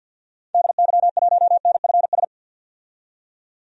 Morse code, heard as a keyed tone, about 52 wpm, 700 Hz. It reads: DY1NFS
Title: DXCC-100 Call Signs - 6 Characters - 50wpm